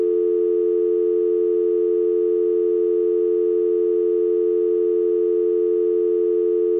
Dial tone (North America), 350 + 440 Hz.
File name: Long beep when the phone is off-hook